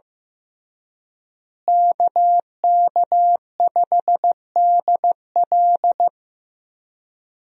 Morse code, heard as a keyed tone, about 15 wpm, 700 Hz. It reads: KK5DL
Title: USA Call Signs - 5 Characters - Repeated 3x - 15wpm